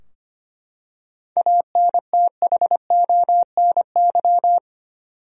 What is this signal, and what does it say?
Morse code, heard as a keyed tone, about 25 wpm, 700 Hz. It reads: ANTHONY
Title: USA Common Names - 25wpm